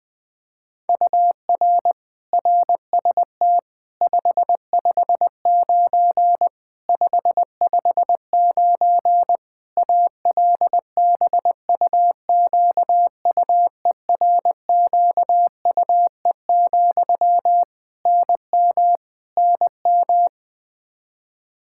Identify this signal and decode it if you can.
Morse code, heard as a keyed tone, about 20 wpm, 700 Hz. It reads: UR RST 559 559 ALBUQUERQUE, NM NM
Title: Practice QSO - Exchange 1 - Beginning - 20wpm